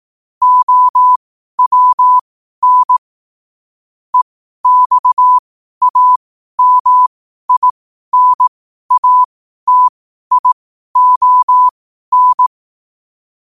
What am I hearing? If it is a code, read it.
Morse code, heard as a keyed tone, about 18 wpm, spaced as for 12 wpm, 1000 Hz. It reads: OWN EXAMINATION
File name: qotdfiles / 12wpm